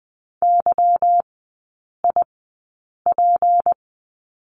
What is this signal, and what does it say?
Morse code, heard as a keyed tone, about 20 wpm, spaced as for 8 wpm, 700 Hz. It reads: YIP
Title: 152 Lesson 25e - Letter p - Three Characters 20wpm